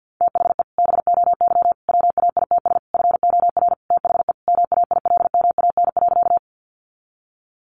Morse code, heard as a keyed tone, about 50 wpm, 700 Hz. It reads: THE BOY WAITS FOR THE MAILMAN.
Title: Easy Phrases 50wpm